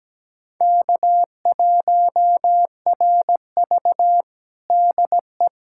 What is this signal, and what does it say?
Morse code, heard as a keyed tone, about 17 wpm, 700 Hz. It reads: K1RV DE
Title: Practice QSO - Exchange 4 - Ending - 17wpm